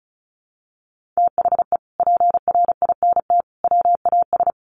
Morse code, heard as a keyed tone, about 34 wpm, 700 Hz. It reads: THE PRINT WAS
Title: Sentences from Top 1400 Words 35wpm